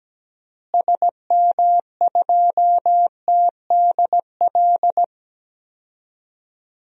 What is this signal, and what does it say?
Morse code, heard as a keyed tone, about 17 wpm, 700 Hz. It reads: SM2TDL
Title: DXCC-100 Call Signs - 6 Characters - Repeated 3x - 17wpm